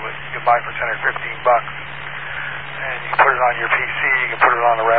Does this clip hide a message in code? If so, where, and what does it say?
Yes: Emergency Alert System (SAME) tones from 0.7 to 4.3 s, heard as harsh buzzing bursts, with speech.